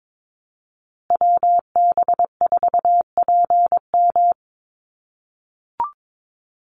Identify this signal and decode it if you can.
Morse code, heard as a keyed tone, about 22 wpm, 700 Hz. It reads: WB4PM
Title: USA Call Signs Repeated 2x 22wpm